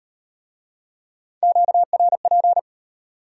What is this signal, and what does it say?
Morse code, heard as a keyed tone, about 38 wpm, 700 Hz. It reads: QRP